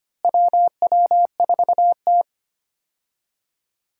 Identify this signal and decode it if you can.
Morse code, heard as a keyed tone, about 25 wpm, 700 Hz. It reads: WW4T
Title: USA Call Signs - 4 Characters - Repeated 3x - 25wpm